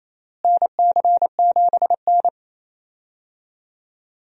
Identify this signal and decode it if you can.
Morse code, heard as a keyed tone, about 28 wpm, 700 Hz. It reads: NC7N
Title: USA Call Signs - 4 Characters - Repeated 3x - 28wpm